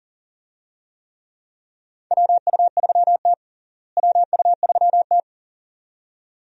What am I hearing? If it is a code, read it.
Morse code, heard as a keyed tone, about 40 wpm, 700 Hz. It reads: WU3T WU3T
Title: USA Call Signs - 4 Characters - Repeated 2x - 40wpm